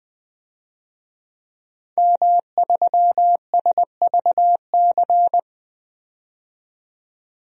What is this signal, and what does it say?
Morse code, heard as a keyed tone, about 20 wpm, 700 Hz. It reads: M3SVC
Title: DXCC-100 Call Signs - 5 Characters - 20wpm